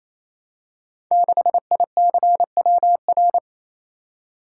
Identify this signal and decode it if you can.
Morse code, heard as a keyed tone, about 27 wpm, 700 Hz. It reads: 6ICWR
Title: Code Groups - 5 Letter:Numbers - 28wpm